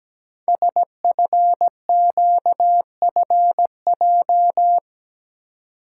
Morse code, heard as a keyed tone, about 17 wpm, 700 Hz. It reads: SFQFJ
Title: Code Groups - 5 Letters - 17wpm